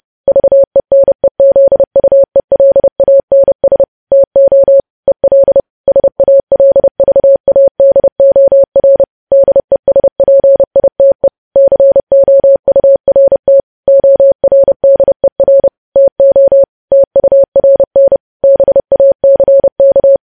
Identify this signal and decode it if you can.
Morse code, heard as a keyed tone, about 30 wpm, 550 Hz. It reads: VENEZUELANS TO EL SALVADOR DESPITE COURT ORDER TO TURN BACK
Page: news headlines in morse code at 30 wpm